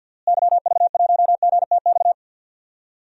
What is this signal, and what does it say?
Morse code, heard as a keyed tone, about 52 wpm, 700 Hz. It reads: YV1GTX
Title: DXCC-100 Call Signs - 6 Characters - Repeated 2x - 50wpm